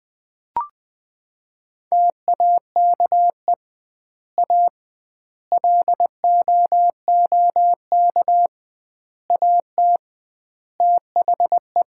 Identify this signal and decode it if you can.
Morse code, heard as a keyed tone, about 20 wpm, 700 Hz. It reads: TAKE A LOOK AT THE
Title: 2x Word-Spacing - Sets of 5 Words - 20wpm